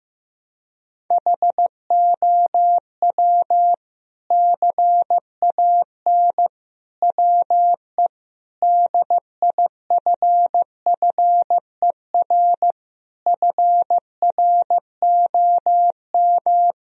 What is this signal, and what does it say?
Morse code, heard as a keyed tone, about 15 wpm, 700 Hz. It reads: HOW CAN WE DIFFER FROM
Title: Sentences from Top 1000 Words - Review Part 1 - 15wpm